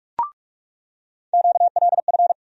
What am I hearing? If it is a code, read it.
Morse code, heard as a keyed tone, about 42 wpm, 700 Hz. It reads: QLF
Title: USA Call Sign Postfixes - 3 Letters 45wpm